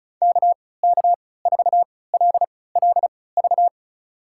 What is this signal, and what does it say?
Morse code, heard as a keyed tone, about 36 wpm, spaced as for 20 wpm, 700 Hz. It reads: KK4LLV
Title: USA Call Signs - 6 Characters - 3x Character Spacing - 35wpm